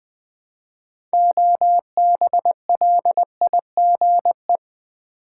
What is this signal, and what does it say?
Morse code, heard as a keyed tone, about 20 wpm, 700 Hz. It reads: OBLIGE
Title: 6 Letter Words - First 3 Letters Given - 20wpm